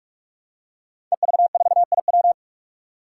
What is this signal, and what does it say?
Morse code, heard as a keyed tone, about 46 wpm, 700 Hz. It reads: EV4IW
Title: Five Challenging Letter-Number Combinations 45wpm